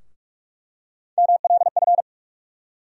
Morse code, heard as a keyed tone, about 46 wpm, 700 Hz. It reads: MLF
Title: USA Call Sign Postfixes - 3 Letters 45wpm